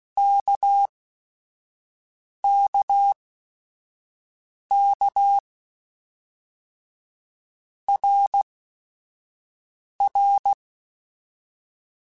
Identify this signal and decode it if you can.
Morse code, heard as a keyed tone, about 16 wpm, spaced as for 5 wpm, 800 Hz. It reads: KKK RR
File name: Basic Sets / Basic Letter Groups Collective7 / RKPX@5